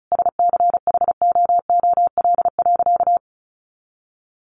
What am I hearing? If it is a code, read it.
Morse code, heard as a keyed tone, about 34 wpm, 700 Hz. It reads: SCHOOL.
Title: Sentences from Top 1100 Words 35wpm